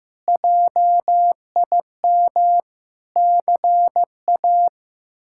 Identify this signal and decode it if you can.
Morse code, heard as a keyed tone, about 15 wpm, 700 Hz. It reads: JIM CA
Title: K1USN SST QSO Practice - 15wpm